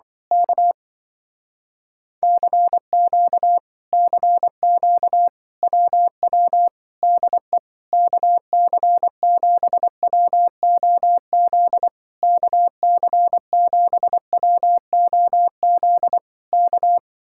Morse code, heard as a keyed tone, about 24 wpm, 700 Hz. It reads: K CQ CQ WW DE KC7WOZ KC7WOZ K
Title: Speed-Racing Common QSO Phrases Encore - 20wpm